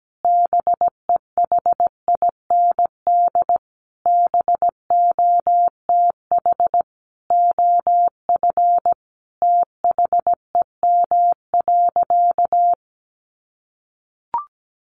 Morse code, heard as a keyed tone, about 17 wpm, 700 Hz. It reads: BEHIND BOTH OF THEM.
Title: Sentences from Top 1400 Words 17wpm